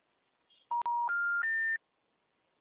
Special information tone.
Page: Braganca - SP